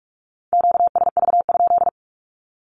Morse code, heard as a keyed tone, about 44 wpm, 700 Hz. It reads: QSV?
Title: Q-Codes Repeated 3x 45wpm